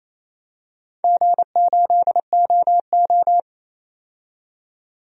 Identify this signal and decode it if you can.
Morse code, heard as a keyed tone, about 28 wpm, 700 Hz. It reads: G8OO
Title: DXCC-100 Call Signs - 4 Characters - 28wpm